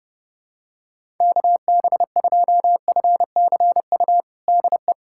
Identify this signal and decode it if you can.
Morse code, heard as a keyed tone, about 30 wpm, 700 Hz. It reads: KB2FCU DE
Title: USA Callsign DE Callsign 30wpm